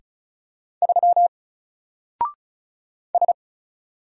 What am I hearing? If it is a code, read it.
Morse code, heard as a keyed tone, about 34 wpm, 700 Hz. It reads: 3 S
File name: Single-Letter-Number-35wpm